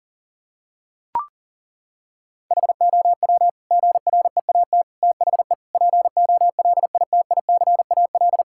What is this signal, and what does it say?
Morse code, heard as a keyed tone, about 37 wpm, 700 Hz. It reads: HOW GREAT THE POLITICAL
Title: Sentences from Top 1000 Words - Review Part 5 - 40wpm